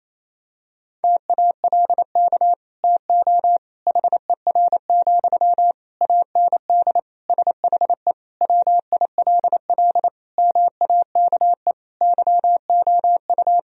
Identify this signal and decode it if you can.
Morse code, heard as a keyed tone, about 28 wpm, 700 Hz. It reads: TALK TO HER, AND SHE WILL MAKE YOU
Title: Sentences from Top 1000 Words - Review Part 3 - 28wpm